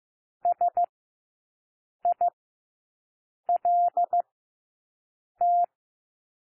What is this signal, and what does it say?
Morse code, heard as a keyed tone, about 15 wpm, spaced as for 6 wpm, 700 Hz. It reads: SILT
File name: cwmp3 / W1 SOUND WORDS 15 wpm